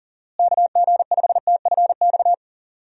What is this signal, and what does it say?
Morse code, heard as a keyed tone, about 37 wpm, 700 Hz. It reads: KG5TFX
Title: USA Call Signs - 6 Characters - Repeated 2x - 40wpm